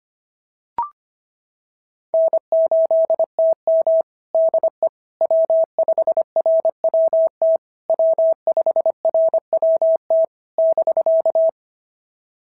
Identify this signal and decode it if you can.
Morse code, heard as a keyed tone, about 24 wpm, 650 Hz. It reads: N8TM DE W5RWT W5RWT <BK>
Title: SOTA QSO Practice 25wpm